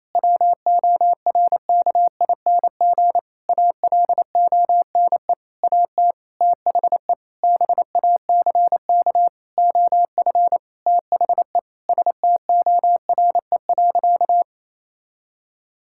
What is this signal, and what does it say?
Morse code, heard as a keyed tone, about 28 wpm, 700 Hz. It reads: WORKING ALONE AT THE BACK OF THE STORE.
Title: Sentences from Top 1000 Words - Review Part 5 - 28wpm